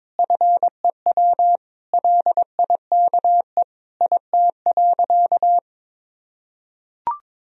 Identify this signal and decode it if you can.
Morse code, heard as a keyed tone, about 22 wpm, 700 Hz. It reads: FEW LIKE IT.
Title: Sentences from Top 1000 Words - Review Part 1 - 22wpm